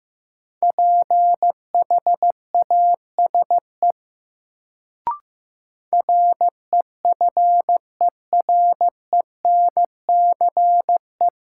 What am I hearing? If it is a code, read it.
Morse code, heard as a keyed tone, about 16 wpm, 700 Hz. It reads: PHASE REFERENCE
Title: Sentences from Top 1200 Words - New Word Drill Alt - 15wpm